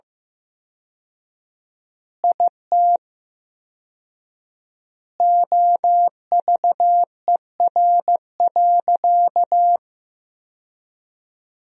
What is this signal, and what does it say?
Morse code, heard as a keyed tone, about 16 wpm, 700 Hz. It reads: IT OVER.
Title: 4x Word-Spacing - Sentences from Top 200 Words - 15wpm